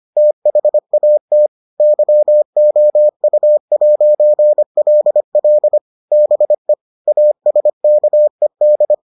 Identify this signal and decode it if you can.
Morse code, heard as a keyed tone, about 25 wpm, 600 Hz. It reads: THAT YOU'LL BE ASKED